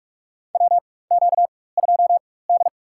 Morse code, heard as a keyed tone, about 46 wpm, spaced as for 21 wpm, 700 Hz. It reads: WQ2D
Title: USA Call Signs - 4 Characters - 4x Character Spacing - 45wpm